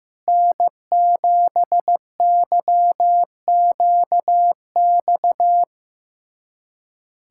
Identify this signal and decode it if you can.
Morse code, heard as a keyed tone, about 15 wpm, 700 Hz. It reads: N7YQX
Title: USA Callsign DE Callsign - 5 Characters - 15wpm